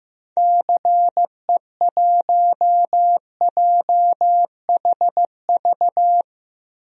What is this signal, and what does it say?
Morse code, heard as a keyed tone, about 15 wpm, 700 Hz. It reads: CE1JHV
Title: DXCC-100 Call Signs - 6 Characters - Repeated 3x - 15wpm